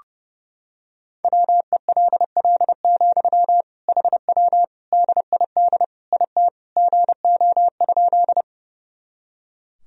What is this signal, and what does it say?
Morse code, heard as a keyed tone, about 30 wpm, 700 Hz. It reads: WELL, HW DID IT GO?
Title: Sentences from Top 200 Words and Abbreviations 30wpm